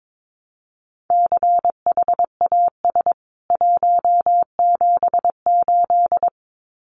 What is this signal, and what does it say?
Morse code, heard as a keyed tone, about 22 wpm, 700 Hz. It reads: CHAS 178
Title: CWT Exchange 22wpm